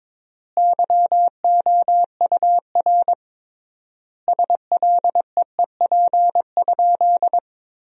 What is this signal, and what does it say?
Morse code, heard as a keyed tone, about 21 wpm, 700 Hz. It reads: YOUR SLEEP?
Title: 3x Word-Spacing - Sentences from Top 500 Words - 22wpm